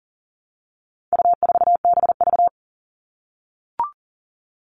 Morse code, heard as a keyed tone, about 40 wpm, 700 Hz. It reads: U4BV
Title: All Cut Numbers - 4-Digit 40wpm